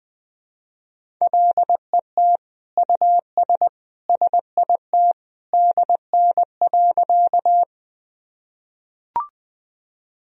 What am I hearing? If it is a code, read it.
Morse code, heard as a keyed tone, about 20 wpm, 700 Hz. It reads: LET US SIT DN.
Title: Sentences from Top 500 Words and Abbreviations 20wpm